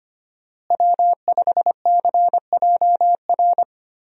Morse code, heard as a keyed tone, about 25 wpm, 700 Hz. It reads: W5CJR
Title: USA Call Signs - 5 Characters - Repeated 3x - 25wpm